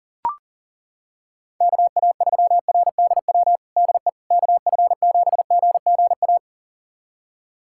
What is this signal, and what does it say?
Morse code, heard as a keyed tone, about 40 wpm, 700 Hz. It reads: KA3RDW DE KF7GGA
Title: USA Callsign DE Callsign 40wpm